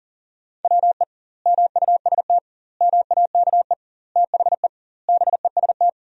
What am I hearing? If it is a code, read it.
Morse code, heard as a keyed tone, about 40 wpm, 700 Hz. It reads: WE MUST MAKE THE BEST
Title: 2x Word-Spacing - Sentences from Top 400 Words - 40wpm